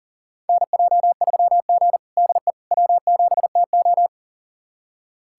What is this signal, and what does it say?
Morse code, heard as a keyed tone, about 40 wpm, 700 Hz. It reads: NJ3G DE W7TO
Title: USA Callsign DE Callsign - 4 Characters - 40wpm